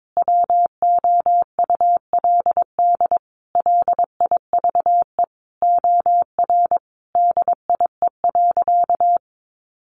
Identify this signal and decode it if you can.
Morse code, heard as a keyed tone, about 22 wpm, 700 Hz. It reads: WOULD LIVE OR DIE.
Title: Sentences from Top 1300 Words 22wpm